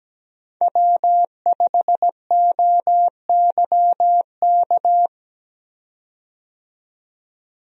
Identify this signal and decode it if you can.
Morse code, heard as a keyed tone, about 17 wpm, 700 Hz. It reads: W5OYK
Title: USA Call Signs - 5 Characters - 17wpm